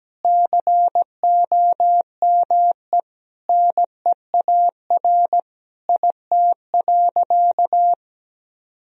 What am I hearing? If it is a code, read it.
Morse code, heard as a keyed tone, about 17 wpm, 700 Hz. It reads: COME NEAR IT.
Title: Sentences from Top 1000 Words - Review Part 1 - 17wpm